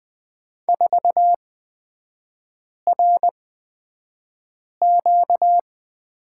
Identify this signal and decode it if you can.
Morse code, heard as a keyed tone, about 20 wpm, spaced as for 5 wpm, 700 Hz. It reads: 4RQ
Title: 236 Lesson 38e - Letter q - Three Characters 20wpm